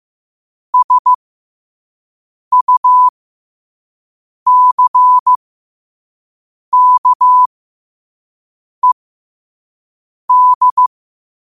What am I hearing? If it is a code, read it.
Morse code, heard as a keyed tone, about 15 wpm, spaced as for 5 wpm, 1000 Hz. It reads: SUCKED
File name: qotdfiles / 5wpm